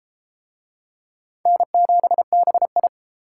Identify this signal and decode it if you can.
Morse code, heard as a keyed tone, about 30 wpm, 700 Hz. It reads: N7BI